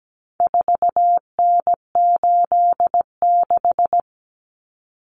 Morse code, heard as a keyed tone, about 17 wpm, 700 Hz. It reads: 4N86
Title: Common Cut Numbers - 4-Digit 17wpm